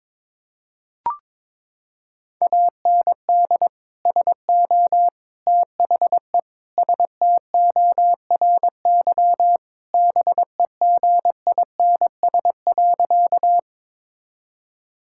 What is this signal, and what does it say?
Morse code, heard as a keyed tone, about 22 wpm, 700 Hz. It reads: AND SO THE STORY BEGINS.
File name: practice / 22wpm / Sentences with 2-Word Phrases - 22wpm